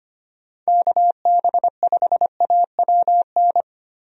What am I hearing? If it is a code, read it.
Morse code, heard as a keyed tone, about 25 wpm, 700 Hz. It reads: KB5AWN
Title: USA Call Signs - 6 Characters - Repeated 3x - 25wpm